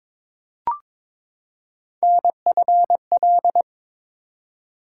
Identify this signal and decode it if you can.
Morse code, heard as a keyed tone, about 22 wpm, 700 Hz. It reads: NFL
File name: ARRL Sections v2 - 22wpm